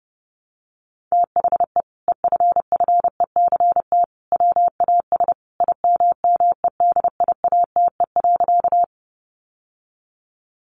Morse code, heard as a keyed tone, about 30 wpm, 700 Hz. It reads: THE EFFECT WAS IMMEDIATE.
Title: Sentences from Top 1200 Words 30wpm